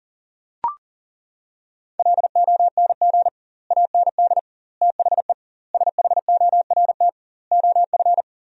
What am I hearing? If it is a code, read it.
Morse code, heard as a keyed tone, about 40 wpm, 700 Hz. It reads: LONG AND THE SHORT OF
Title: 2x Word-Spacing - Sentences from Top 400 Words - 40wpm